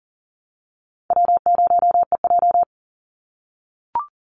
Morse code, heard as a keyed tone, about 40 wpm, 700 Hz. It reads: W0EJ
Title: Easy Call Signs - 40wpm